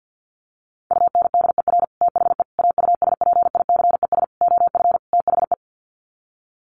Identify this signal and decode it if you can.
Morse code, heard as a keyed tone, about 50 wpm, 700 Hz. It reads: UNDER THE AUSPICES OF THE
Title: Sets of 5 Words Encore2 50wpm